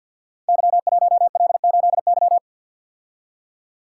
Morse code, heard as a keyed tone, about 51 wpm, 700 Hz. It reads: Y1L8Y
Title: Code Groups - 5 Letter:Numbers - 50wpm